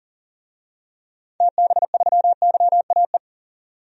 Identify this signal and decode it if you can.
Morse code, heard as a keyed tone, about 40 wpm, 700 Hz. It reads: TB3YAE
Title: DXCC-100 Call Signs - 6 Characters - 40wpm